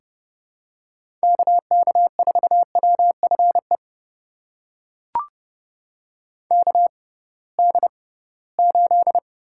Morse code, heard as a keyed tone, about 30 wpm, spaced as for 19 wpm, 700 Hz. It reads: KK4WFE KD8
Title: USA Call Signs - 6 Characters - 6x Character Spacing - 30wpm